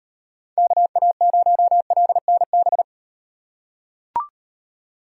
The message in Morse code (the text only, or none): KA0LNB